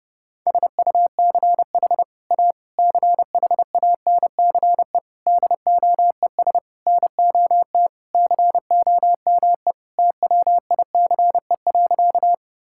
Morse code, heard as a keyed tone, about 30 wpm, 700 Hz. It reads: SUCH A CHANCE DOES NOT COME TWICE.
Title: Sentences from Top 1600 Words 30wpm